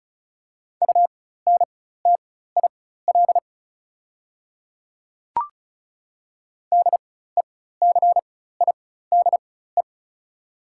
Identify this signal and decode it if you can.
Morse code, heard as a keyed tone, about 36 wpm, spaced as for 16 wpm, 700 Hz. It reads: UNTIL DECIDE
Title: 4x Character Spacing - Top 500 Words - 35wpm